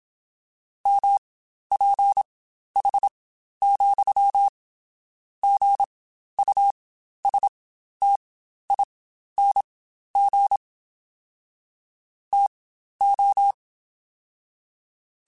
Morse code, heard as a keyed tone, about 27 wpm, spaced as for 12 wpm, 800 Hz. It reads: MPH,GUSTING TO
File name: Spaced Characters / Spaced Characters-mp3s / Weather4 21 sp @25